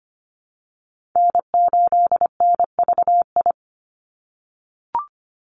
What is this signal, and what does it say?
Morse code, heard as a keyed tone, about 25 wpm, 700 Hz. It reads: N8NVI
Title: USA Call Signs 25wpm